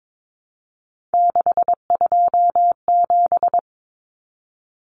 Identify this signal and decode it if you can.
Morse code, heard as a keyed tone, about 22 wpm, 700 Hz. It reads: 627
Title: Common Cut Numbers - 3-Digit 22wpm